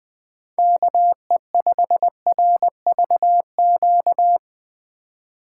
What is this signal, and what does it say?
Morse code, heard as a keyed tone, about 20 wpm, 700 Hz. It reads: KE5RVQ
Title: USA Call Signs - 6 Characters - 3x Character Spacing - 20wpm